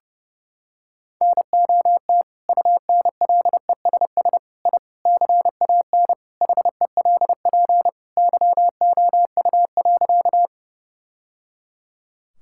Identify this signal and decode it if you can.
Morse code, heard as a keyed tone, about 30 wpm, 700 Hz. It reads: NOT UNLESS I CAN HELP YOU.
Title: Sentences from Top 1000 Words - Review Part 6 - 30wpm